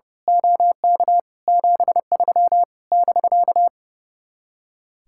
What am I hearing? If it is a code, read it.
Morse code, heard as a keyed tone, about 28 wpm, 700 Hz. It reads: OK 73 <BK>
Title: Speed-Racing Common QSO Phrases - 30wpm